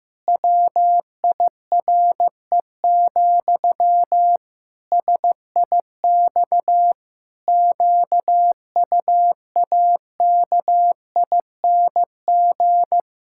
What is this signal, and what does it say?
Morse code, heard as a keyed tone, about 15 wpm, 700 Hz. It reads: WIRE, SIX QUAKING
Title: Pangram Challenge v2 - 15wpm